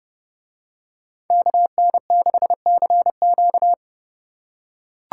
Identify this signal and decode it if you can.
Morse code, heard as a keyed tone, about 30 wpm, 700 Hz. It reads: KN6CQ